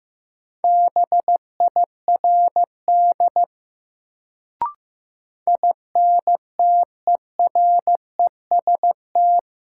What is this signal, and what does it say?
Morse code, heard as a keyed tone, about 15 wpm, 700 Hz. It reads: BIRD INTEREST
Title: Sentences from Top 400 Words - New Word Drill - 15wpm